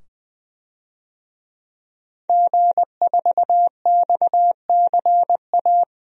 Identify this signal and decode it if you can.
Morse code, heard as a keyed tone, about 20 wpm, 700 Hz. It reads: G4XCA
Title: DXCC-100 Call Signs - 5 Characters - 20wpm